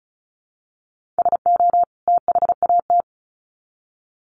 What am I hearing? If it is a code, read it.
Morse code, heard as a keyed tone, about 34 wpm, 700 Hz. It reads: SO THAT
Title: Sets of 2 Words 35wpm